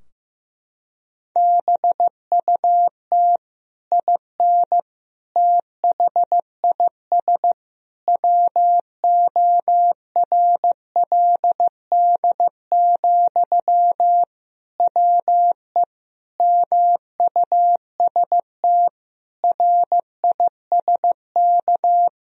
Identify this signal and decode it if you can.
Morse code, heard as a keyed tone, about 15 wpm, 700 Hz. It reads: BUT IN THIS WORLD, WE MUST RISK
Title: Sentences from Top 1000 Words - Review Part 4 - 15wpm